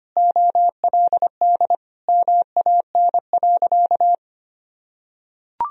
Morse code, heard as a keyed tone, about 25 wpm, 700 Hz. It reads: OLD MAN.
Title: Sentences from Top 200 Words and Abbreviations 25wpm